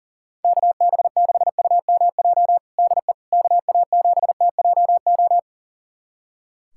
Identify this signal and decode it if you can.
Morse code, heard as a keyed tone, about 37 wpm, 700 Hz. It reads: KB6UMJ DE KA7TJO
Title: USA Callsign DE Callsign - 6 Characters - 40wpm